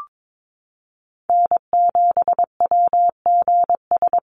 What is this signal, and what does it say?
Morse code, heard as a keyed tone, about 22 wpm, 700 Hz. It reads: N7WGS
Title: USA Call Signs Repeated 3x 22wpm